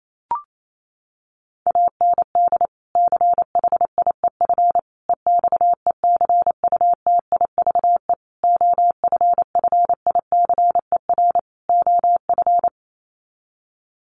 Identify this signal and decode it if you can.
Morse code, heard as a keyed tone, about 28 wpm, 700 Hz. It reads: AND CHIEF EXECUTIVE OFFICER OF
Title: Sets of 5 Words Encore2 28wpm